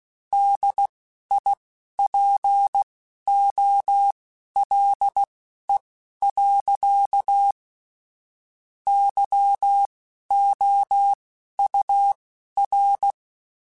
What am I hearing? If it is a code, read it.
Morse code, heard as a keyed tone, about 16 wpm, spaced as for 11 wpm, 800 Hz. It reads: DIPOLE. YOUR
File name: (spaced) / QSOs Spaced-mp3s / WD0FBT 1 @15 sp